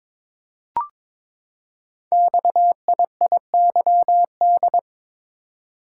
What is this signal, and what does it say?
Morse code, heard as a keyed tone, about 23 wpm, 700 Hz. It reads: XIIYD
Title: Code Groups - 5 Letter:Numbers - 22wpm